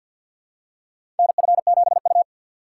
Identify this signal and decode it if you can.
Morse code, heard as a keyed tone, about 53 wpm, 700 Hz. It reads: NF7U
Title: USA Call Signs - 2x Character Spacing - 50wpm